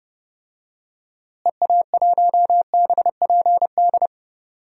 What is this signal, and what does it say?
Morse code, heard as a keyed tone, about 30 wpm, 700 Hz. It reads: EA1BPD
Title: DXCC-100 Call Signs - 6 Characters - Repeated 2x - 30wpm